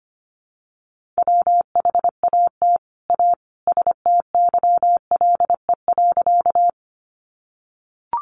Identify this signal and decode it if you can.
Morse code, heard as a keyed tone, about 25 wpm, 700 Hz. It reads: WHAT A STYLE.
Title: Sentences from Top 800 Words 25wpm